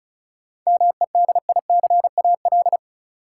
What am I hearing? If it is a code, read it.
Morse code, heard as a keyed tone, about 34 wpm, 700 Hz. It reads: MEDICAL
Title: Sentences from Top 800 Words - New Word Drill - 35wpm